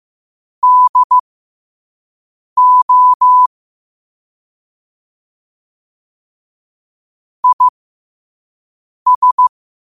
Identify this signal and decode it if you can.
Morse code, heard as a keyed tone, about 15 wpm, spaced as for 5 wpm, 1000 Hz. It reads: DO IS
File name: qotdfiles / 5wpm